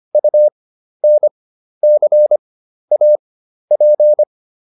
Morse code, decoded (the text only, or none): UNCAP